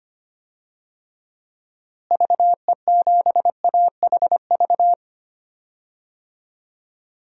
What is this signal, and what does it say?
Morse code, heard as a keyed tone, about 25 wpm, 700 Hz. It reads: VE7AHV
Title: DXCC-100 Call Signs - 6 Characters - Repeated 2x - 25wpm